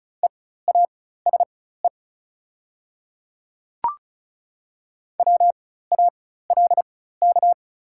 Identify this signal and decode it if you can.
Morse code, heard as a keyed tone, about 36 wpm, spaced as for 16 wpm, 690 Hz. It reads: EASE WALK
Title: 4x Character Spacing - Top 300 Words - 35wpm